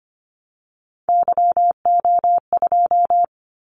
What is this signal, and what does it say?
Morse code, heard as a keyed tone, about 25 wpm, 700 Hz. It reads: YO2
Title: Callsign Prefixes for Top 25 DXCC - 25wpm